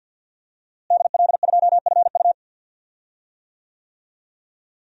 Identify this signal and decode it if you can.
Morse code, heard as a keyed tone, about 45 wpm, 700 Hz. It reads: DL2FU
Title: DXCC-100 Call Signs - 5 Characters - Repeated 2x - 50wpm